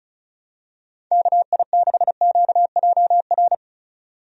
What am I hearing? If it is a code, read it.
Morse code, heard as a keyed tone, about 36 wpm, 700 Hz. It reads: KI6QJR
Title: USA Call Signs - 6 Characters - 35wpm